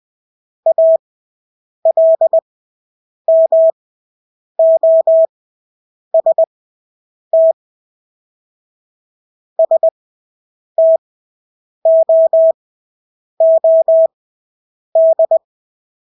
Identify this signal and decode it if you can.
Morse code, heard as a keyed tone, about 20 wpm, spaced as for 8 wpm, 650 Hz. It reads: ALMOST STOOD
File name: Engelske ord 20/8 wpm